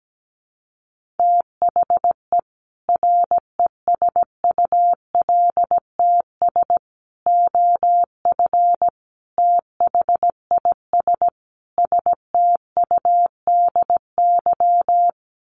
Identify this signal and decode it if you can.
Morse code, heard as a keyed tone, about 17 wpm, 700 Hz. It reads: THE RESULTS OF THIS STUDY
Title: Sets of 5 Words 17wpm